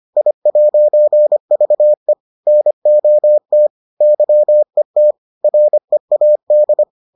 Morse code, heard as a keyed tone, about 25 wpm, 600 Hz. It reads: I'VE NOT YET READ